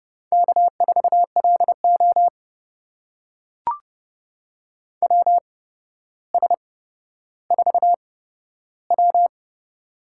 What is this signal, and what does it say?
Morse code, heard as a keyed tone, about 30 wpm, spaced as for 13 wpm, 700 Hz. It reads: K4LO WS4W
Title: USA Call Signs - 4 Characters - 8x Character Spacing - 30wpm